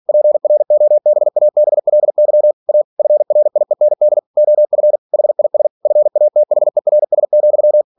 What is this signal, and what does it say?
Morse code, heard as a keyed tone, about 46 wpm, 600 Hz. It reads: PROBABLY A FRIEND OF HIS FATHERS,